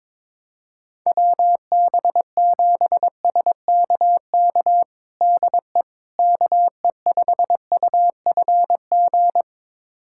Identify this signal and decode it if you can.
Morse code, heard as a keyed tone, about 22 wpm, 700 Hz. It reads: WB7SKK DE KE5UFG
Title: USA Callsign DE Callsign - 6 Characters - 22wpm